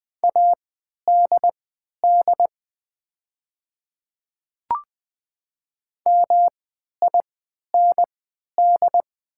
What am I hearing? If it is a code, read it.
Morse code, heard as a keyed tone, about 20 wpm, spaced as for 11 wpm, 700 Hz. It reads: ADD MIND
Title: 3x Character Spacing - Top 500 Words - 20wpm